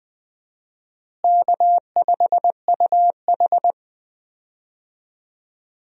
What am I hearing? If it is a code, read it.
Morse code, heard as a keyed tone, about 20 wpm, 700 Hz. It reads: K5UH
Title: USA Call Signs - 4 Characters - Repeated 2x - 20wpm